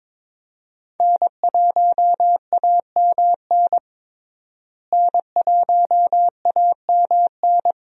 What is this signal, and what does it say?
Morse code, heard as a keyed tone, about 21 wpm, 700 Hz. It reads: N1AMN N1AMN
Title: USA Call Signs - 5 Characters - Repeated 3x - 22wpm